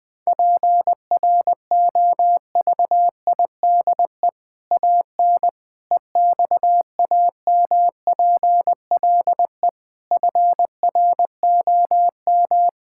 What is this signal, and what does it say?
Morse code, heard as a keyed tone, about 20 wpm, 700 Hz. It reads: PROVIDE AN EXAMPLE FROM
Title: Sentences from Top 1000 Words - Review Part 5 - 20wpm